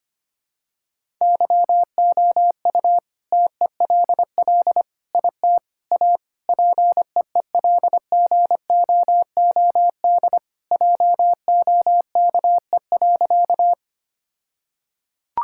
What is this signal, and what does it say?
Morse code, heard as a keyed tone, about 25 wpm, 700 Hz. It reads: YOU TELL IT A PEELGOOD JOKE.
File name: practice / 25wpm / All-Age Appropriate Jokes - 25wpm